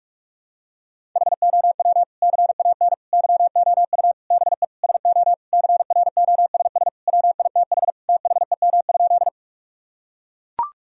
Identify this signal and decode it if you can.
Morse code, heard as a keyed tone, about 42 wpm, 700 Hz. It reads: HOW CAN YOU BE SO CROSS WITH THEM?
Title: Sentences from Top 1000 Words - Review Part 1 - 45wpm